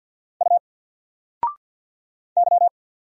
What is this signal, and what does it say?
Morse code, heard as a keyed tone, about 50 wpm, 700 Hz. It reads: U Y
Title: Single Character (Letter:Number:Punctuation:Prosign) 50wpm